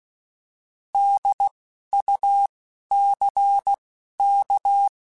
Morse code, heard as a keyed tone, about 16 wpm, 800 Hz. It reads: D U C K
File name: Alphabetical / Alphabetical mp3s / Ds @10